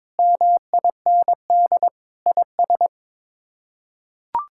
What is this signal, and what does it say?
Morse code, heard as a keyed tone, about 22 wpm, 700 Hz. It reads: MIND IS
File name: practice / 22wpm / Sets of 8 Words - 22wpm